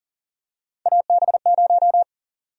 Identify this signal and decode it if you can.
Morse code, heard as a keyed tone, about 40 wpm, 700 Hz. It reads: AB0
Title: USA Call Sign Prefixes - 40wpm